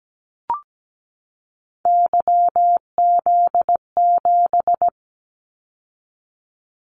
Morse code, heard as a keyed tone, about 17 wpm, 700 Hz. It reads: YZ7
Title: Callsign Prefixes for Top 25 DXCC - 17wpm